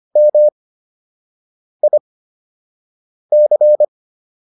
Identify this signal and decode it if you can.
Morse code, heard as a keyed tone, about 25 wpm, spaced as for 6 wpm, 600 Hz. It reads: MIC